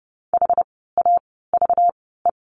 Morse code, heard as a keyed tone, about 30 wpm, spaced as for 17 wpm, 700 Hz. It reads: HAVE
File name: course / 30:10wpm / 237 Lesson 38f - Letter q - Words 30wpm